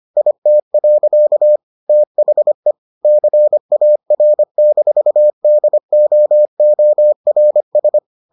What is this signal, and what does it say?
Morse code, heard as a keyed tone, about 25 wpm, 600 Hz. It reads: IT. THE CAR-DOORS